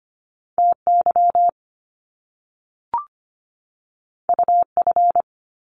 Morse code, heard as a keyed tone, about 25 wpm, 700 Hz. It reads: TY UF
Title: Top Two Letter Combinations in Words 25wpm